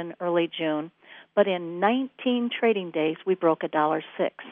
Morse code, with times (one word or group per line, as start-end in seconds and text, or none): none